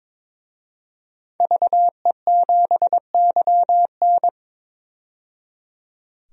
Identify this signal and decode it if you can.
Morse code, heard as a keyed tone, about 21 wpm, 700 Hz. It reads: VE7YN
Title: DXCC-100 Call Signs - 5 Characters - 22wpm